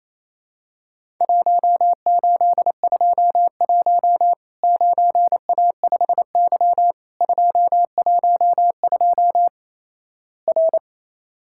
Morse code, heard as a keyed tone, about 28 wpm, 700 Hz. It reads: 1821 9A5Y 212 R
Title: Worked All Europe DX Contest - QTC Exchange - 28wpm